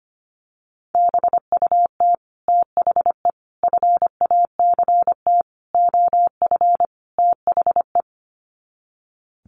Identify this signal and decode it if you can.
Morse code, heard as a keyed tone, about 25 wpm, 700 Hz. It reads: BUT THE FACT OF THE
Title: Sets of 5 Words Encore2 25wpm